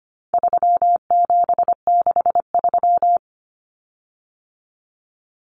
Morse code, heard as a keyed tone, about 25 wpm, 700 Hz. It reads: 3763
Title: Common Cut Numbers - 4-Digit 25wpm